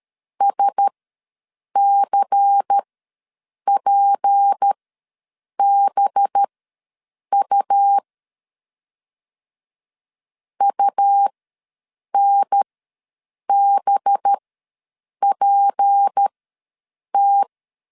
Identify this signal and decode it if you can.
Morse code, heard as a keyed tone, about 13 wpm, spaced as for 7 wpm, 800 Hz. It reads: SCPBU UNBPT
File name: LEZIONI / 6 PKX0 / 6 P ws 12wpm 3117